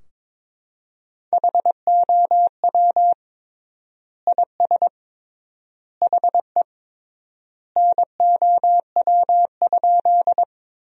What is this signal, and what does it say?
Morse code, heard as a keyed tone, about 22 wpm, 700 Hz. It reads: HOW IS HE NOW?
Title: 3x Word-Spacing - Sentences from Top 100 Words - 22wpm